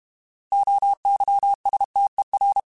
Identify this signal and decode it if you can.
Morse code, heard as a keyed tone, about 32 wpm, 800 Hz. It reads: OYSTER
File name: Alphabetical / Alphabetical mp3s / Os @30